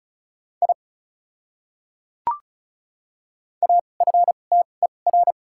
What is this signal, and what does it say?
Morse code, heard as a keyed tone, about 36 wpm, spaced as for 25 wpm, 700 Hz. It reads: I AFTER
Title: 2x Character Spacing - Top 300 Words - 35wpm